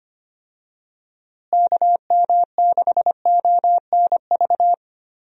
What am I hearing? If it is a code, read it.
Morse code, heard as a keyed tone, about 25 wpm, 700 Hz. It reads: KM6ONV
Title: USA Call Signs - 6 Characters - Repeated 3x - 25wpm